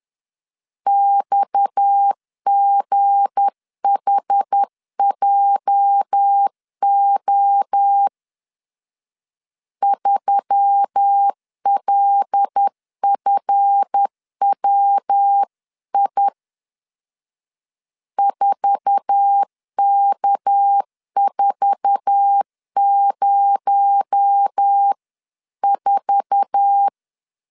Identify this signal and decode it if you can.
Morse code, heard as a keyed tone, about 10 wpm, 800 Hz. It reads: XGHJO 3LFWI 4K404
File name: LEZIONI / 8 YJ45 / 8 YJ4 10wpm 2633